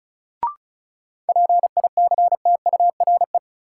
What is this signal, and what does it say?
Morse code, heard as a keyed tone, about 34 wpm, 700 Hz. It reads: PICTURE
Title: Sentences from Top 500 Words - New Word Drill - 35wpm